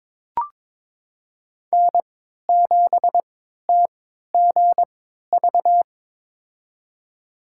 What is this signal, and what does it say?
Morse code, heard as a keyed tone, about 22 wpm, spaced as for 13 wpm, 700 Hz. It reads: N7TGV
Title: USA Call Signs - 5 Characters - 3x Character Spacing - 22wpm